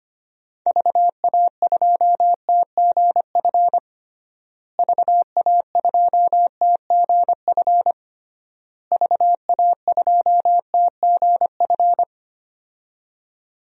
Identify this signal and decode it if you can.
Morse code, heard as a keyed tone, about 25 wpm, 700 Hz. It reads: VA2TGF VA2TGF VA2TGF
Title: DXCC-100 Call Signs - 6 Characters - Repeated 3x - 25wpm